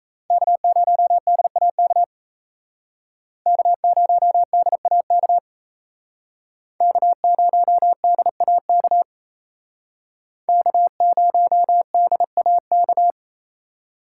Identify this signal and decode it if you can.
Morse code, heard as a keyed tone, about 35 wpm, 700 Hz. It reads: K0DAK K0DAK K0DAK K0DAK